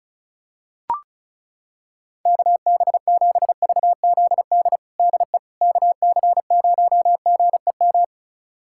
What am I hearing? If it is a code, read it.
Morse code, heard as a keyed tone, about 36 wpm, 700 Hz. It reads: KB7VZD DE KC0GEM
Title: USA Callsign DE Callsign 35wpm